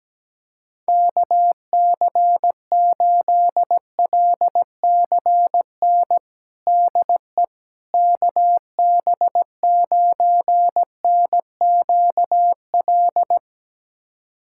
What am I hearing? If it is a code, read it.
Morse code, heard as a keyed tone, about 17 wpm, 700 Hz. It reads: KC8LCN DE KB9NQL
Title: USA Callsign DE Callsign - 6 Characters - 17wpm